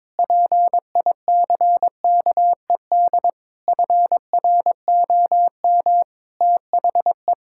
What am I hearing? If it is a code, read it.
Morse code, heard as a keyed tone, about 22 wpm, 700 Hz. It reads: PICKED FROM THE
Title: Pangram Challenge v2 - 22wpm